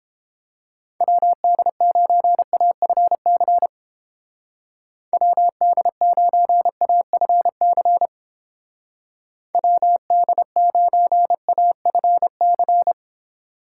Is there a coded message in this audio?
Yes: Morse code from 1.0 to 12.9 s, heard as a keyed tone, about 29 wpm, 700 Hz. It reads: WD9AFC WD9AFC WD9AFC